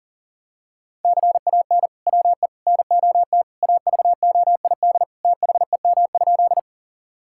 Morse code, heard as a keyed tone, about 40 wpm, 700 Hz. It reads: CAN WE NOT AVOID THEM?
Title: Sentences from Top 1000 Words - Review Part 6 - 40wpm